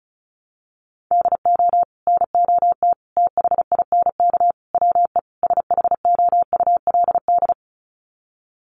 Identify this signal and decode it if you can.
Morse code, heard as a keyed tone, about 34 wpm, 700 Hz. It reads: DO NOT THINK WE SHOULD
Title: Sets of 5 Words Encore 35wpm